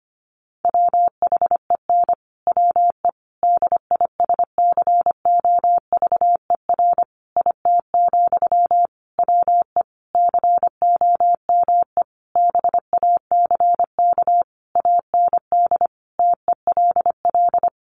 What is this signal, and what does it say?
Morse code, heard as a keyed tone, about 25 wpm, 700 Hz. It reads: WHEN WE DISCOVER IT, WE COME BACK AND TELL
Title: Sentences from Top 1100 Words 25wpm